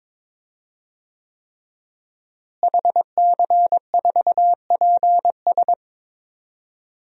Morse code, heard as a keyed tone, about 22 wpm, 700 Hz. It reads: HC4PS
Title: DXCC-100 Call Signs - 5 Characters - Repeated 3x - 22wpm